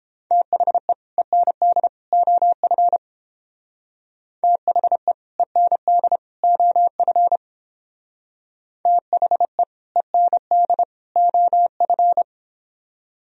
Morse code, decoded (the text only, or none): THE END OF THE END OF THE END OF